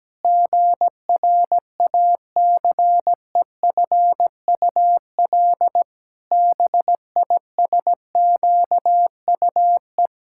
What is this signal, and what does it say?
Morse code, heard as a keyed tone, about 17 wpm, 700 Hz. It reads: GRACEFUL BISQUE
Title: Pangram Challenge v2 - 17wpm